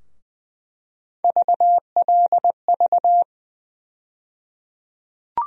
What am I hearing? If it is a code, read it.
Morse code, heard as a keyed tone, about 20 wpm, 700 Hz. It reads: VLV
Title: USA Call Sign Postfixes - 3 Letters 20wpm